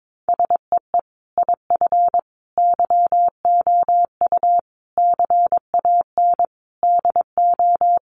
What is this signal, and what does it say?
Morse code, heard as a keyed tone, about 22 wpm, 700 Hz. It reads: SEE IF YOU CAN DO
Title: Sentences from Top 1000 Words 22wpm